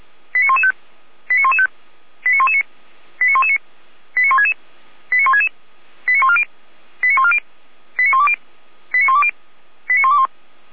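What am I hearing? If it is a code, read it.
Five-tone selective call (ZVEI), heard as quick runs of five tones, tones of about 70 ms. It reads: 89115 89115 89119 89119 89166 89166 89140 89140 89120 89120 89121
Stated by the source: ZVEI five-tone selective call 89099, received on an SDR, from the Feuerwehr Orlamünde fire brigade.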